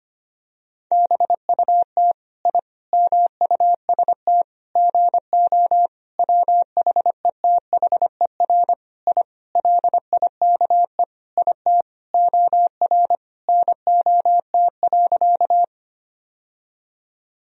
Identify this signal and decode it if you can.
Morse code, heard as a keyed tone, about 25 wpm, 700 Hz. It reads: BUT I MUST GO WHETHER I LIKE IT OR NOT.
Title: Sentences from Top 1000 Words - Review Part 3 - 25wpm